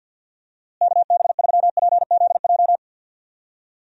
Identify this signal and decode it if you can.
Morse code, heard as a keyed tone, about 52 wpm, 700 Hz. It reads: KB3PZJ
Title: USA Call Signs - 6 Characters - Repeated 2x - 50wpm